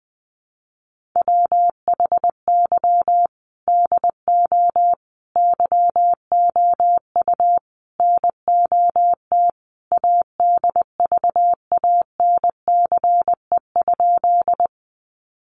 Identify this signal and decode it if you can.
Morse code, heard as a keyed tone, about 20 wpm, 700 Hz. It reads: WHY DO YOU NOT ADVANCE?
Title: Sentences from Top 1400 Words 20wpm